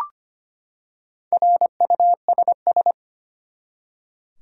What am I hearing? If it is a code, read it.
Morse code, heard as a keyed tone, about 25 wpm, 700 Hz. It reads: RUSS
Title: USA Common Names - Max 4 Letters - 25wpm